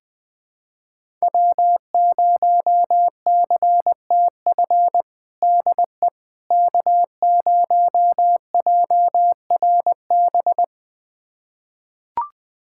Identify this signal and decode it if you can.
Morse code, heard as a keyed tone, about 20 wpm, 700 Hz. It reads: W0CTF DE K0JRB
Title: USA Callsign DE Callsign - 5 Characters - 20wpm